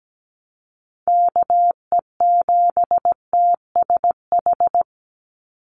Morse code, heard as a keyed tone, about 17 wpm, 700 Hz. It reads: KE7TSH
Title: USA Call Signs 17wpm